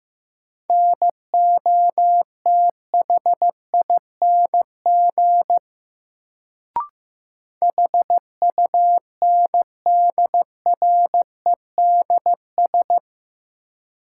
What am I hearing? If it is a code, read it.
Morse code, heard as a keyed tone, about 16 wpm, 700 Hz. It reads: NOTHING HUNDREDS
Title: Sets of 4 Words - New Word Drill - 15wpm